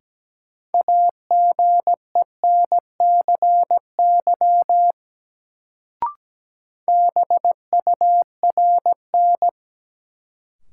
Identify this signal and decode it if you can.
Morse code, heard as a keyed tone, about 17 wpm, 700 Hz. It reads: AGENCY BURN
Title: Sentences from Top 1300 Words - New Word Drill Alt - 17wpm